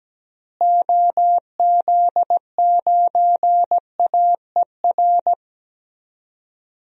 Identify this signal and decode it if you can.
Morse code, heard as a keyed tone, about 17 wpm, 700 Hz. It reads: OZ9AER
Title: DXCC-100 Call Signs - 6 Characters - Repeated 2x - 17wpm